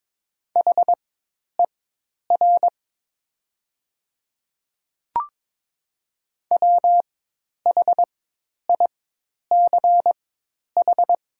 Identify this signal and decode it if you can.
Morse code, heard as a keyed tone, about 22 wpm, spaced as for 10 wpm, 700 Hz. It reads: HER WHICH
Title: 4x Character Spacing - Top 100 Words - 22wpm